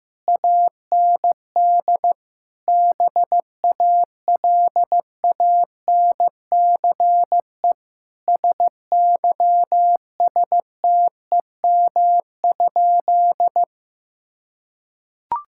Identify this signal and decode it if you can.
Morse code, heard as a keyed tone, about 15 wpm, 700 Hz. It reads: AND BALANCE SYSTEM?
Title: Sentences from Top 1000 Words - Review Part 6 - 15wpm